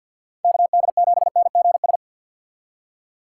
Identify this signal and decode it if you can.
Morse code, heard as a keyed tone, about 60 wpm, 700 Hz. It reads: KD7NGS
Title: Speed Racing - USA 2x3 Call Signs 50wpm